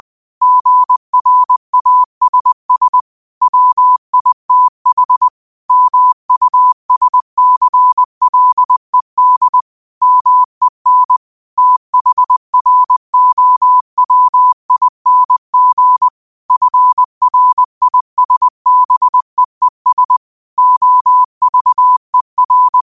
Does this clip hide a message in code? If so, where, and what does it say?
Yes: Morse code from 0.4 to 22.8 s, heard as a keyed tone, about 20 wpm, 1000 Hz. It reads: GRASS WITH MUSCLED MEN THROWING FRISBEES OVER